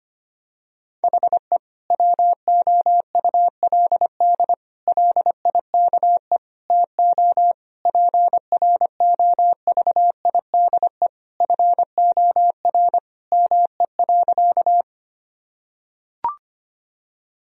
Morse code, heard as a keyed tone, about 25 wpm, 700 Hz. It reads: HE WOULD LIKE TO PROVIDE FOR ME.
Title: Sentences from Top 1000 Words - Review Part 3 - 25wpm